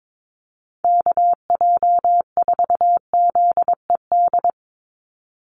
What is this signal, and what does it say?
Morse code, heard as a keyed tone, about 22 wpm, 700 Hz. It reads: KJ4ZED
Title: USA Call Signs 22wpm